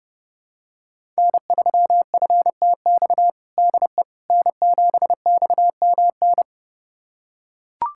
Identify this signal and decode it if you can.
Morse code, heard as a keyed tone, about 30 wpm, 700 Hz. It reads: N3FTX DE N7XMN
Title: USA Callsign DE Callsign - 5 Characters - 30wpm